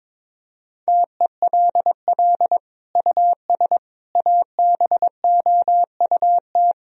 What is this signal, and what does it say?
Morse code, heard as a keyed tone, about 22 wpm, 700 Hz. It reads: TELL US ABOUT
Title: Sentences from Top 1000 Words - Review Part 3 - 22wpm